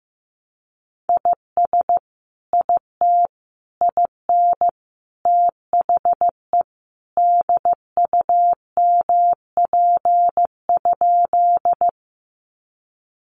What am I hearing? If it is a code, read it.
Morse code, heard as a keyed tone, about 15 wpm, 700 Hz. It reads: IS IT IN THE DUMP?
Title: Sentences from Top 1300 Words 15wpm